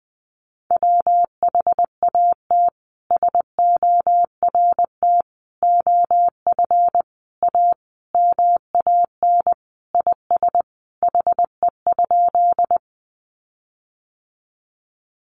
Morse code, heard as a keyed tone, about 20 wpm, 700 Hz. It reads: WHAT SORT OF A MAN IS HE?
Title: Sentences from Top 1300 Words 20wpm